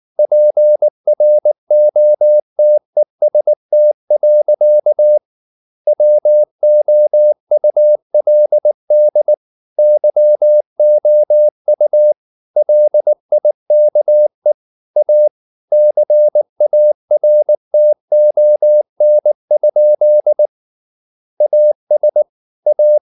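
Morse code, heard as a keyed tone, about 19 wpm, 600 Hz. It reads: PROTEST. WOULD YOU LIKE A CARTON? AS A
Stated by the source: Morse code at 19 wpm from CW med Gnister, reading Ken Follett's novel Never in English.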